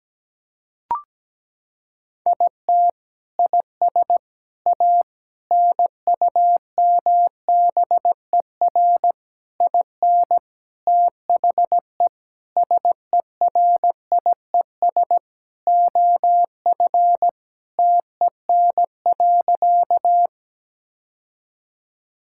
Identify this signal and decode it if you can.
Morse code, heard as a keyed tone, about 17 wpm, 700 Hz. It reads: IT IS A NUMBER IN THE SERIES OF TEN.
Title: Sentences from Top 1000 Words - Review Part 5 - 17wpm